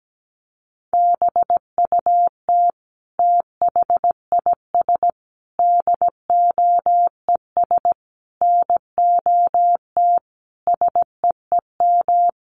Morse code, heard as a keyed tone, about 17 wpm, 700 Hz. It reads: BUT THIS DOES NOT SEEM
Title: Sentences from Top 1400 Words 17wpm